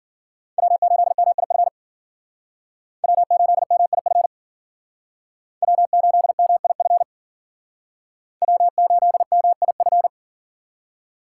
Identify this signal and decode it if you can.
Morse code, heard as a keyed tone, about 54 wpm, 700 Hz. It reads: W8MIF W8MIF W8MIF W8MIF